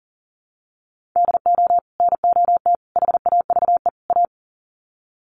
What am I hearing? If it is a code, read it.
Morse code, heard as a keyed tone, about 40 wpm, 700 Hz. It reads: DO NOT HAVE A
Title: Sets of 4 Words 40wpm